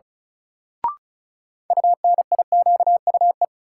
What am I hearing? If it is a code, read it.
Morse code, heard as a keyed tone, about 34 wpm, 700 Hz. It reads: UNIQUE
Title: Sentences from Top 1000 Words - Review Part 7 - New Word Drill - 35wpm